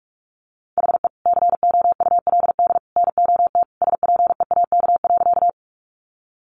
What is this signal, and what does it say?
Morse code, heard as a keyed tone, about 44 wpm, 700 Hz. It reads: HE COULD NOT SPEAK.
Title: Sentences from Top 1100 Words 45wpm